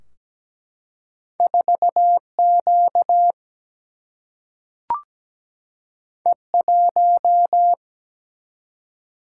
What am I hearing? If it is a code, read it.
Morse code, heard as a keyed tone, about 17 wpm, 700 Hz. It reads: 4Q E1
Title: Single-Letter Single-Number 17wpm